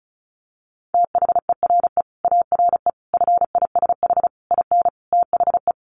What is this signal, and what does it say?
Morse code, heard as a keyed tone, about 34 wpm, 700 Hz. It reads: THERE ARE FISH IN THE
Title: Sentences from Top 1100 Words 35wpm